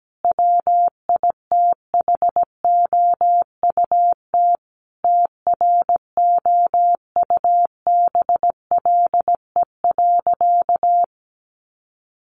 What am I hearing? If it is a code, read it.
Morse code, heard as a keyed tone, about 17 wpm, 700 Hz. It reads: WITHOUT TROUBLE.
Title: Sentences from Top 1200 Words 17wpm